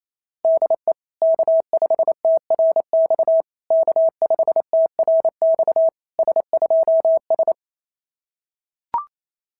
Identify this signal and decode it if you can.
Morse code, heard as a keyed tone, about 27 wpm, 650 Hz. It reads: DE K5TRX K5TRX S2S
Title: SOTA QSO Practice 28wpm